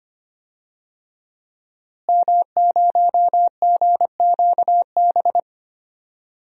Morse code, heard as a keyed tone, about 25 wpm, 700 Hz. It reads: M0GQB
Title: DXCC-100 Call Signs - 5 Characters - Repeated 3x - 25wpm